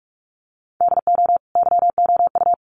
Morse code, heard as a keyed tone, about 46 wpm, 700 Hz. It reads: DO YOU
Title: Sentences from Top 1100 Words 45wpm